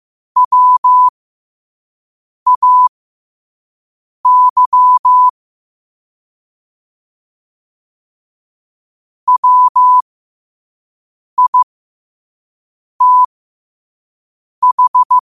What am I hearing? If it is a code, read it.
Morse code, heard as a keyed tone, about 15 wpm, spaced as for 5 wpm, 1000 Hz. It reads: WAY WITH
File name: qotdfiles / 5wpm